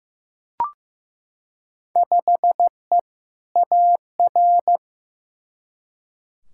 Morse code, heard as a keyed tone, about 15 wpm, 700 Hz. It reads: 5E AR
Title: Field Day Exchanges Encore - 15wpm